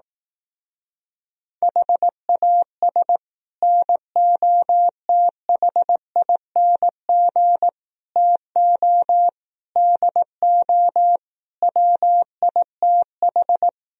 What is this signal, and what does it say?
Morse code, heard as a keyed tone, about 18 wpm, 700 Hz. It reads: HAS NOTHING TO DO WITH
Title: Speed-Racing - Sets of 5 Words 15wpm